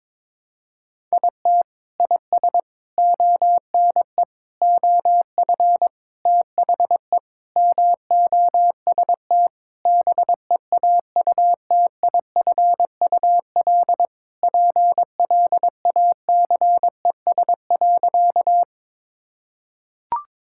Morse code, heard as a keyed tone, about 22 wpm, 700 Hz. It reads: IT IS ONE OF THE MOST BEAUTIFUL PLACES.
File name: practice / 22wpm / Sentences with 3-Word Phrases - 22wpm